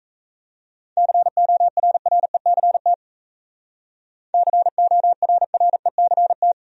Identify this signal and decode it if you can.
Morse code, heard as a keyed tone, about 34 wpm, 700 Hz. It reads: CORRECT CORRECT
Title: Speed-Racing Top 500 Words 28wpm